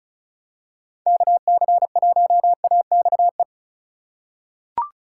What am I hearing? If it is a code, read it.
Morse code, heard as a keyed tone, about 36 wpm, 700 Hz. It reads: KC1AXE
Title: USA Call Signs - 6 Characters - 3x Character Spacing - 35wpm